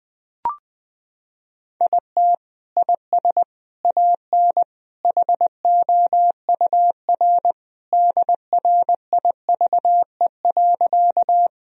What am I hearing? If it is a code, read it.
Morse code, heard as a keyed tone, about 21 wpm, 700 Hz. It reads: IT IS AN HOUR DRIVE.
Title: Sentences from Top 1000 Words - Review Part 2 - 20wpm